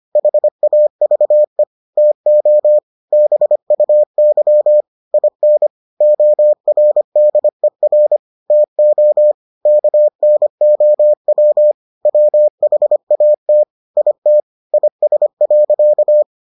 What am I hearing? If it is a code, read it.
Morse code, heard as a keyed tone, about 25 wpm, 600 Hz. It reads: HAVE TO BUY IN ORDER TO KNOW WHAT IT IS.